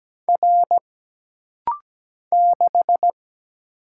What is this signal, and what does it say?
Morse code, heard as a keyed tone, about 17 wpm, 700 Hz. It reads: R 6
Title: Single Letter-Number Alt - Rapid-Fire 17wpm